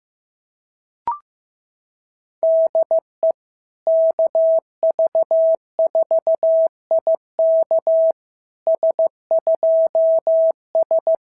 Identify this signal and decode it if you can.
Morse code, heard as a keyed tone, about 15 wpm, 690 Hz. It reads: DE KV4IK S2S
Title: SOTA QSO Practice 15wpm